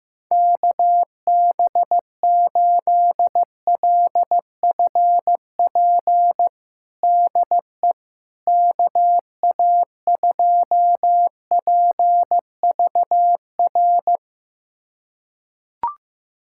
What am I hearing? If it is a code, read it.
Morse code, heard as a keyed tone, about 15 wpm, 700 Hz. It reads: KB8LFP DE KA2PVR
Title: USA Callsign DE Callsign - 6 Characters - 15wpm